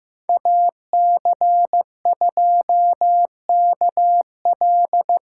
Morse code, heard as a keyed tone, about 15 wpm, 700 Hz. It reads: AC2KL
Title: USA Call Signs - 5 Characters - 4x Character Spacing - 15wpm